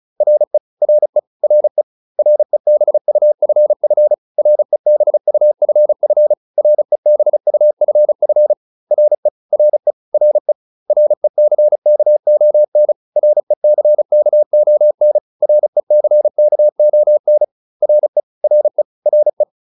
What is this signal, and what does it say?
Morse code, heard as a keyed tone, about 34 wpm, 600 Hz. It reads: RE RE RE REBUFF REBUFF REBUFF RE RE RE RECKON RECKON RECKON RE RE RE
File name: RE6 35